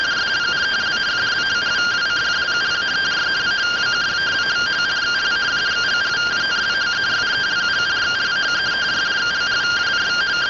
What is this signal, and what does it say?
Two-tone teleprinter signal, about 170 Hz shift.
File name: findings / 202 coronal iodine 3mm.gen-15 rtty rx